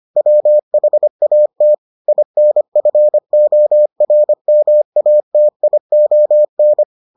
Morse code, heard as a keyed tone, about 25 wpm, 600 Hz. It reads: WHAT INFORMATION